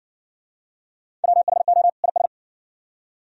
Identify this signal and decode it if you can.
Morse code, heard as a keyed tone, about 52 wpm, 700 Hz. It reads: WHO IS